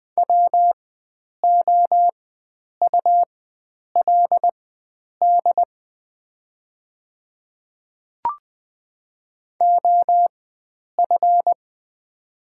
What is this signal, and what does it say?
Morse code, heard as a keyed tone, about 20 wpm, spaced as for 9 wpm, 700 Hz. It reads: WOULD OF
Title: 4x Character Spacing - Top 100 Words - 20wpm